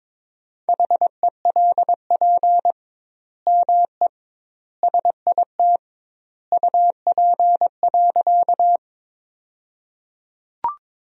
Morse code, heard as a keyed tone, about 21 wpm, 700 Hz. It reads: HELP ME SIT UP.
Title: 2x Word-Spacing - Sentences from Top 500 Words - 22wpm